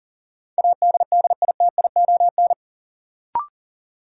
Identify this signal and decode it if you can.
Morse code, heard as a keyed tone, about 39 wpm, 700 Hz. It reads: ADDITION
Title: Sentences from Top 1500 Words - New Word Drill Alt - 40wpm